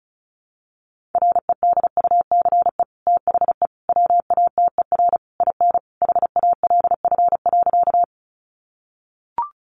Morse code, heard as a keyed tone, about 34 wpm, 700 Hz. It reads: REDUCE THE WATER IN HALF.
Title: Sentences from Top 1100 Words 35wpm